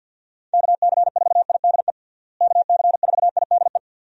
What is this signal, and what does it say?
Morse code, heard as a keyed tone, about 52 wpm, 700 Hz. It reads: KC4IDE KC4IDE
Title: USA Call Signs - 6 Characters - Repeated 2x - 50wpm